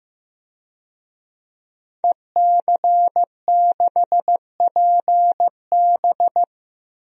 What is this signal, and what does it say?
Morse code, heard as a keyed tone, about 15 wpm, 700 Hz. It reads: EC6PB
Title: DXCC-100 Call Signs - 5 Characters - 15wpm